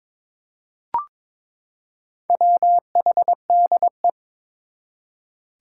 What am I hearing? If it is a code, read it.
Morse code, heard as a keyed tone, about 22 wpm, 700 Hz. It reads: WHDE
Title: Four Challenging Letter-Number Combinations 22wpm